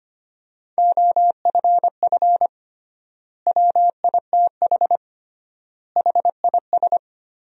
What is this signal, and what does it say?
Morse code, heard as a keyed tone, about 25 wpm, 700 Hz. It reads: OFF WITH HIS
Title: 3x Word-Spacing - Sentences from Top 300 Words - 25wpm